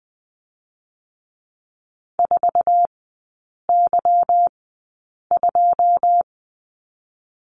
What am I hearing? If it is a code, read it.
Morse code, heard as a keyed tone, about 20 wpm, 690 Hz. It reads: 4 Y 2
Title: 145 Lesson 24e - Letter y - Three Characters 20wpm